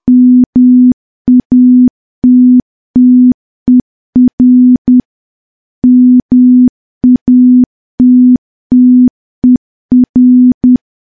Morse code, heard as a keyed tone, about 10 wpm, 250 Hz. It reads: MATTER MATTER
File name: morse / morse-training-10WPM